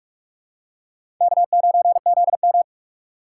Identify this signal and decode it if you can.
Morse code, heard as a keyed tone, about 43 wpm, 690 Hz. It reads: K9ZM
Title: USA Call Signs - 4x Character Spacing - 45wpm